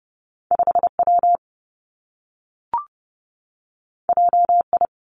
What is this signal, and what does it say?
Morse code, heard as a keyed tone, about 30 wpm, 700 Hz. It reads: 5 W J I
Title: Two Challenging Letter-Number Combinations 30wpm